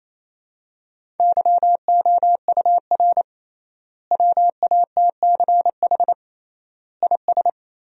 Morse code, heard as a keyed tone, about 28 wpm, 700 Hz. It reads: YOUR WATCH IS
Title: 3x Word-Spacing - Sentences from Top 400 Words - 28wpm